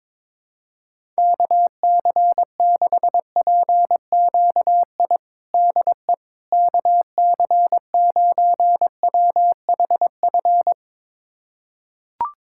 Morse code, heard as a keyed tone, about 21 wpm, 700 Hz. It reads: KC6PQI DE KC9WHF
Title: USA Callsign DE Callsign - 6 Characters - 22wpm